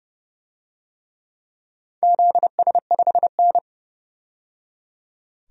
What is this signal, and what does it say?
Morse code, heard as a keyed tone, about 30 wpm, 700 Hz. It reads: ZS5N
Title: DXCC-100 Call Signs - 4 Characters - 30wpm